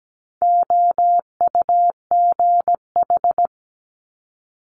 Morse code, heard as a keyed tone, about 17 wpm, 700 Hz. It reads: OUGH
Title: Multi-letter Phonemes Quick Fire 17wpm